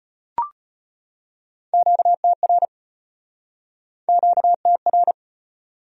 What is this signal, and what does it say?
Morse code, heard as a keyed tone, about 35 wpm, 700 Hz. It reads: QTR QTR